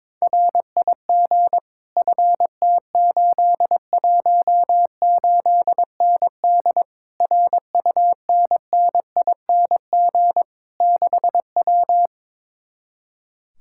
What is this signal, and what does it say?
Morse code, heard as a keyed tone, about 22 wpm, 700 Hz. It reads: RIG FT818ND RUNNING 6W
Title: Speed-Racing Common QSO Phrases Encore - 22wpm